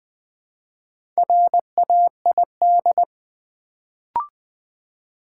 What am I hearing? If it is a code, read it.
Morse code, heard as a keyed tone, about 20 wpm, 700 Hz. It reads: RAID
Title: Build A Word - 1 Letter At A Time - 4-Letter Words - 20wpm